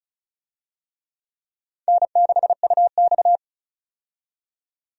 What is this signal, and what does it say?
Morse code, heard as a keyed tone, about 34 wpm, 700 Hz. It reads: N6UX
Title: USA Call Signs - 4 Characters - 35wpm